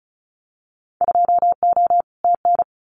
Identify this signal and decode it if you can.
Morse code, heard as a keyed tone, about 34 wpm, 700 Hz. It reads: 2O TN
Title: Winter Field Day Exchanges 35wpm